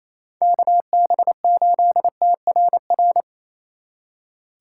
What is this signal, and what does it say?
Morse code, heard as a keyed tone, about 34 wpm, 700 Hz. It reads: KB8TRR
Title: Speed Racing - USA 2x3 Call Signs 28wpm